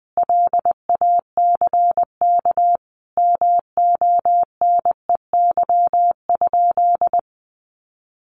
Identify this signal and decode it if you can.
Morse code, heard as a keyed tone, about 20 wpm, 700 Hz. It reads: LACK MONEY?
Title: Sentences from Top 1100 Words 20wpm